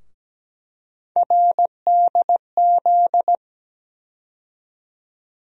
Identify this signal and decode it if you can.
Morse code, heard as a keyed tone, about 17 wpm, 700 Hz. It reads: RDZ
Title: USA Call Sign Postfixes - 3 Letters 17wpm